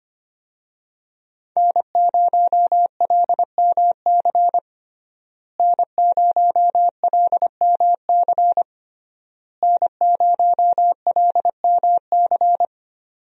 Morse code, heard as a keyed tone, about 25 wpm, 700 Hz. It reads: N0LMC N0LMC N0LMC
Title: USA Call Signs - 5 Characters - Repeated 3x - 25wpm